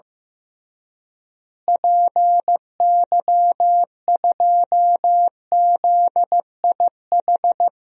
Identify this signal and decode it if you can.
Morse code, heard as a keyed tone, about 15 wpm, 700 Hz. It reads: PY2ZIH
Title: DXCC-100 Call Signs - 6 Characters - Repeated 3x - 15wpm